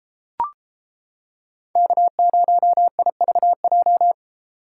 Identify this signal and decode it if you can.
Morse code, heard as a keyed tone, about 31 wpm, 700 Hz. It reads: K0IVJ